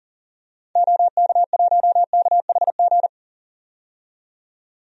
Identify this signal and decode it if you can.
Morse code, heard as a keyed tone, about 40 wpm, 700 Hz. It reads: OK1KHG
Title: DXCC-100 Call Signs - 6 Characters - 40wpm